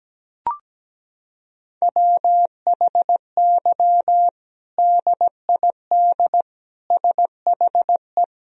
Morse code, heard as a keyed tone, about 17 wpm, 700 Hz. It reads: WHY DID SHE
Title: Sentences from Top 1000 Words - Review Part 3 - 17wpm